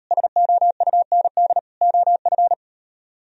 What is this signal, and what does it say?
Morse code, heard as a keyed tone, about 38 wpm, 700 Hz. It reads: SOUND OF